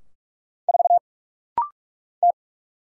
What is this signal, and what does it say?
Morse code, heard as a keyed tone, about 44 wpm, 700 Hz. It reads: 4 T
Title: Single Letter-Number - Rapid-Fire 45wpm